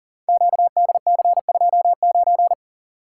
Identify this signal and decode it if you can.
Morse code, heard as a keyed tone, about 42 wpm, 700 Hz. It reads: QDC29
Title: Code Groups - 5 Letter:Numbers - 40wpm